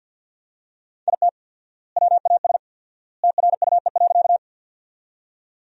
Morse code, heard as a keyed tone, about 50 wpm, 700 Hz. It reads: IT WAS TRUE.
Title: 4x Word-Spacing - Sentences from Top 400 Words - 50wpm